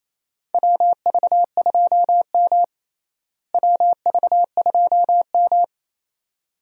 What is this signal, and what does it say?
Morse code, heard as a keyed tone, about 28 wpm, 700 Hz. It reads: WV2M WV2M
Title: USA Call Signs - 4 Characters - Repeated 2x - 28wpm